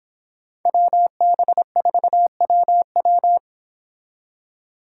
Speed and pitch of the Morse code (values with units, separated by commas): 25 wpm, 700 Hz